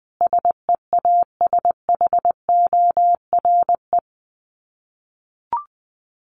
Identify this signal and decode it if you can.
Morse code, heard as a keyed tone, about 20 wpm, 700 Hz. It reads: SEASHORE
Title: Dit Challenge 20wpm